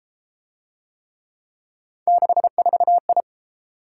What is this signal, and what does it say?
Morse code, heard as a keyed tone, about 29 wpm, 700 Hz. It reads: 64I